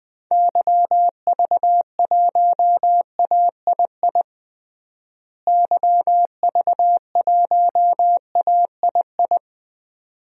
Morse code, heard as a keyed tone, about 20 wpm, 700 Hz. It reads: YV1AII YV1AII
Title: DXCC-100 Call Signs - 6 Characters - Repeated 2x - 20wpm